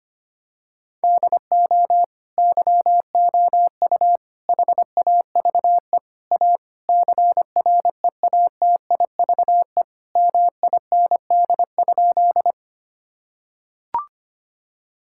Morse code, heard as a keyed tone, about 25 wpm, 700 Hz. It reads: DO YOU HAVE A CREATIVE MIND?
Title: Sentences from Top 1000 Words - Review Part 5 - 25wpm